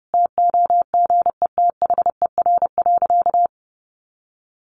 Morse code, heard as a keyed tone, about 30 wpm, 700 Hz. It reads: TOGETHER.
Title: Sentences from Top 1100 Words 30wpm